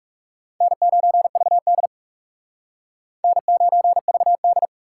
Morse code, heard as a keyed tone, about 39 wpm, 700 Hz. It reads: N9VD N9VD